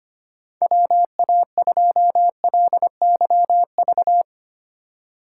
Morse code, heard as a keyed tone, about 25 wpm, 700 Hz. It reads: WA2LYV
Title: USA Call Signs - 6 Characters - 2x Character Spacing - 25wpm